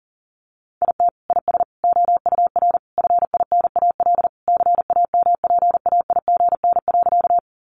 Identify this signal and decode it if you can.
Morse code, heard as a keyed tone, about 40 wpm, 700 Hz. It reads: IT IS OUR FINAL CAMPAIGN.
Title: Sentences from Top 1200 Words 40wpm